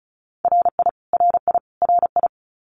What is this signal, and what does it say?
Morse code, heard as a keyed tone, about 34 wpm, 700 Hz. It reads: RI RI RI
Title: US States Repeated 3x 35wpm